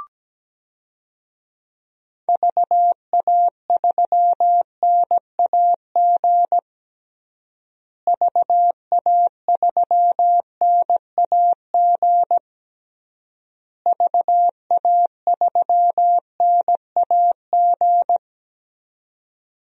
Morse code, heard as a keyed tone, about 17 wpm, 700 Hz. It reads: VA3NAG VA3NAG VA3NAG
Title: DXCC-100 Call Signs - 6 Characters - Repeated 3x - 17wpm